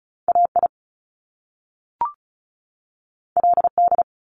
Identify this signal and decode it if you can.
Morse code, heard as a keyed tone, about 36 wpm, 700 Hz. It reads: AI LD
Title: Top Two Letter Combinations in Words 35wpm